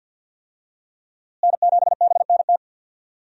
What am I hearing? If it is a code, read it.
Morse code, heard as a keyed tone, about 46 wpm, 700 Hz. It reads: N7BNT
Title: USA Call Signs - 5 Characters - Repeated 2x - 50wpm